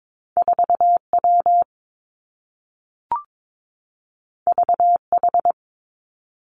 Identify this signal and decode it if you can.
Morse code, heard as a keyed tone, about 22 wpm, 700 Hz. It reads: 4 W V H
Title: Two Challenging Letter-Number Combinations 22wpm